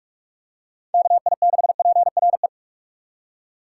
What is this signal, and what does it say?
Morse code, heard as a keyed tone, about 46 wpm, 700 Hz. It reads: KI6PRE
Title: USA Call Signs - 2x Character Spacing - 45wpm